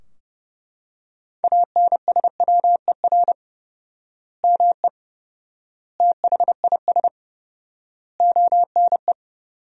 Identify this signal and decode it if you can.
Morse code, heard as a keyed tone, about 30 wpm, 700 Hz. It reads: ANSWER ME THIS ONE
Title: 4x Word-Spacing - Sentences from Top 400 Words - 30wpm